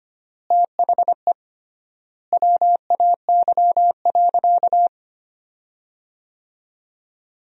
Morse code, heard as a keyed tone, about 25 wpm, 700 Hz. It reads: THE WAY.
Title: 3x Word-Spacing - Sentences from Top 300 Words - 25wpm